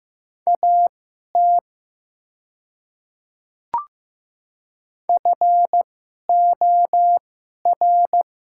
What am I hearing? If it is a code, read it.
Morse code, heard as a keyed tone, about 15 wpm, spaced as for 11 wpm, 700 Hz. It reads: AT FOR
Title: 2x Character Spacing - Top 500 Words - 15wpm